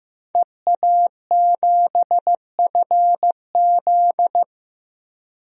Morse code, heard as a keyed tone, about 15 wpm, 700 Hz. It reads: EA7FZ
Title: DXCC-100 Call Signs - 5 Characters - 15wpm